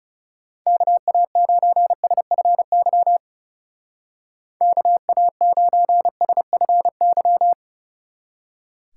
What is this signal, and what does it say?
Morse code, heard as a keyed tone, about 36 wpm, 700 Hz. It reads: KA9SFY KA9SFY
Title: Speed Racing - USA 2x3 Call Signs 30wpm